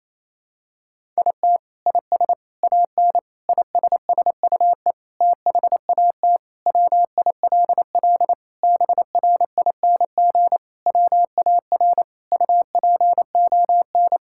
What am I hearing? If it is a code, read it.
Morse code, heard as a keyed tone, about 28 wpm, 700 Hz. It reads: IT IS AN ISSUE THAT WILL BRING WAR UPON
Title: Sentences from Top 1000 Words - Review Part 6 - 28wpm